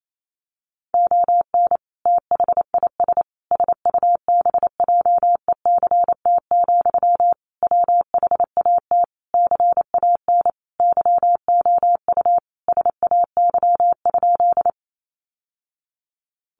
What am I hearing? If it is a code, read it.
Morse code, heard as a keyed tone, about 28 wpm, 700 Hz. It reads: ON THIS SUBJECT, WHAT CAN YOU SAY?
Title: Sentences from Top 800 Words 28wpm